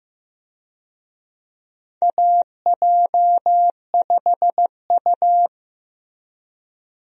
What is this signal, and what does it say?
Morse code, heard as a keyed tone, about 16 wpm, 700 Hz. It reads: AJ5U
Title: USA Call Signs - 4 Characters - Repeated 3x - 15wpm